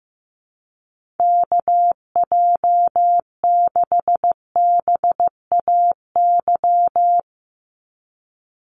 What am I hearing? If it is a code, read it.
Morse code, heard as a keyed tone, about 15 wpm, 700 Hz. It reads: KJ6BAY
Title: USA Call Signs 15wpm